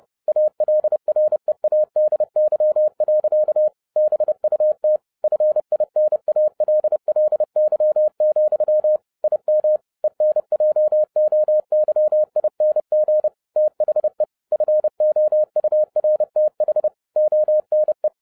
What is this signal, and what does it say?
Morse code, heard as a keyed tone, about 30 wpm, 610 Hz. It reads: ALREADY. BUT FINALLY, IM ENJOYING THE FOURTH ONE
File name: 2012-01-mp3 / n11-I DIDNT MISS THE SOLAR CYCLE 24-30wpm